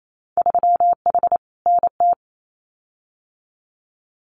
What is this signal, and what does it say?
Morse code, heard as a keyed tone, about 28 wpm, 700 Hz. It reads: 3H NT
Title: Winter Field Day Exchanges Encore 28wpm